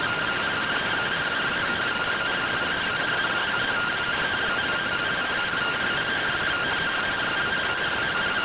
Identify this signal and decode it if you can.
Two-tone teleprinter signal, about 170 Hz shift.
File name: findings / MR AXIAL 71mm 3mm.gen-29 rtty rx